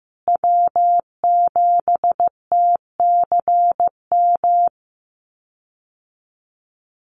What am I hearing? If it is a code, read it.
Morse code, heard as a keyed tone, about 15 wpm, 700 Hz. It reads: W7TCM
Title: USA Call Signs Repeated 2x 15wpm